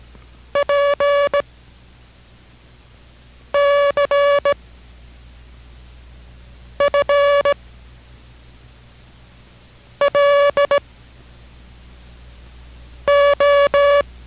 Morse code, heard as a keyed tone, about 17 wpm, 570 Hz. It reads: P C F L O